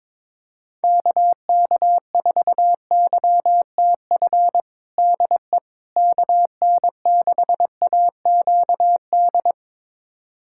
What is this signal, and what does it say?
Morse code, heard as a keyed tone, about 22 wpm, 700 Hz. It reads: KK4YTF DE KN6AQD
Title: USA Callsign DE Callsign - 6 Characters - 22wpm